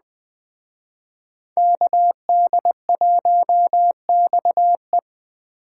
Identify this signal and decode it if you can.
Morse code, heard as a keyed tone, about 20 wpm, 700 Hz. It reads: KD1XE
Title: Speed Racing - USA 2x2 Call Signs 17wpm